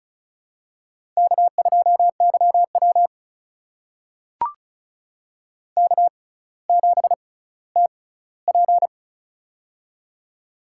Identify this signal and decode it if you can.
Morse code, heard as a keyed tone, about 36 wpm, spaced as for 18 wpm, 700 Hz. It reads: K2YW K7TP
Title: USA Call Signs - 4 Characters - 6x Character Spacing - 35wpm